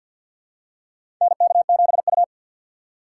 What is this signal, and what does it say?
Morse code, heard as a keyed tone, about 52 wpm, 700 Hz. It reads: NK7U
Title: K1USN SST QSO Practice - 50wpm